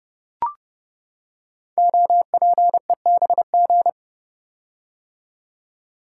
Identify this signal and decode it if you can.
Morse code, heard as a keyed tone, about 29 wpm, 700 Hz. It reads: OPEBG
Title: Code Groups - 5 Letter:Numbers - 30wpm